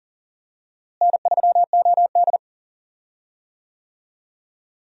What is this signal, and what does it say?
Morse code, heard as a keyed tone, about 39 wpm, 700 Hz. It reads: N3OD
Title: USA Call Signs - 4 Characters - Repeated 2x - 40wpm